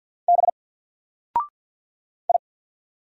Morse code, heard as a keyed tone, about 48 wpm, 700 Hz. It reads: B I
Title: Single Character (Letter:Number:Punctuation:Prosign) Alt 50wpm